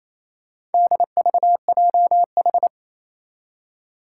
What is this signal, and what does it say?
Morse code, heard as a keyed tone, about 28 wpm, 700 Hz. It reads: DVJH
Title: Four Challenging Letter-Number Combinations 28wpm